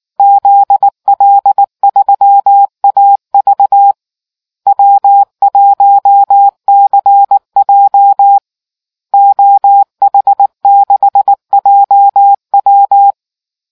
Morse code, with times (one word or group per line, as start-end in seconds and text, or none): 0.2-3.9 ZL3AV
4.7-8.4 W1CJ
9.1-13.1 OH6JW